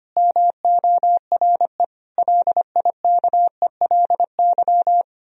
Morse code, heard as a keyed tone, about 31 wpm, 700 Hz. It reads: MORE LIKELY